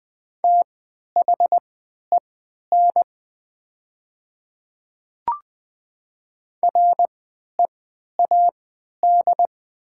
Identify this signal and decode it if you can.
Morse code, heard as a keyed tone, about 20 wpm, spaced as for 11 wpm, 700 Hz. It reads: THEN READ
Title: 3x Character Spacing - Top 200 Words - 20wpm